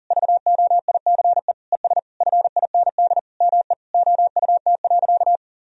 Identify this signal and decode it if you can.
Morse code, heard as a keyed tone, about 40 wpm, 700 Hz. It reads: VOICE ES FIND ME OUT.
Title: Sentences from Top 500 Words and Abbreviations 40wpm